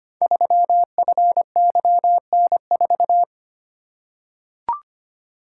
Morse code, heard as a keyed tone, about 25 wpm, 700 Hz. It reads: 3FYN4
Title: Code Groups - 5 Letter:Numbers - 25wpm